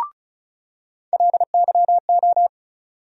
Morse code, heard as a keyed tone, about 34 wpm, 700 Hz. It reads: LYO
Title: USA Call Sign Postfixes - 3 Letters 35wpm